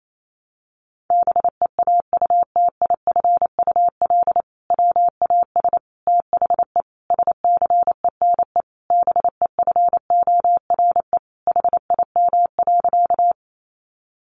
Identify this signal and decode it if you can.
Morse code, heard as a keyed tone, about 28 wpm, 700 Hz. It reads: BEAUTIFUL WAS THE SCENE BEFORE HIM.
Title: Sentences from Top 1100 Words 28wpm